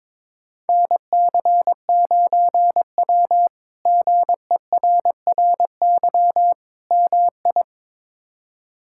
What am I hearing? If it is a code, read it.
Morse code, heard as a keyed tone, about 22 wpm, 700 Hz. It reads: NC9W GERRY MI
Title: K1USN SST QSO Practice - 22wpm